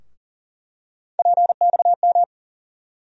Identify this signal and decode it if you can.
Morse code, heard as a keyed tone, about 40 wpm, 700 Hz. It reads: PXM
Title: USA Call Sign Postfixes - 3 Letters 40wpm